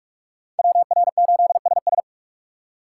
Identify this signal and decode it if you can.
Morse code, heard as a keyed tone, about 41 wpm, 700 Hz. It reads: WR8SS